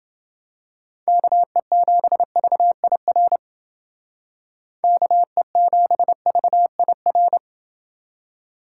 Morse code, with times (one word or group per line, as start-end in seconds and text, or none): 1.1-3.4 KE7VIR
4.8-7.4 KE7VIR